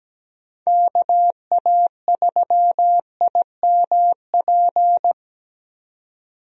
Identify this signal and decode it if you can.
Morse code, heard as a keyed tone, about 17 wpm, 700 Hz. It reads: KA3IMP
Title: USA Call Signs - 6 Characters - 17wpm